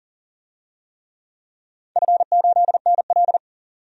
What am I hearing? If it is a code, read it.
Morse code, heard as a keyed tone, about 40 wpm, 700 Hz. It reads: F8NL
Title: DXCC-100 Call Signs - 4 Characters - 40wpm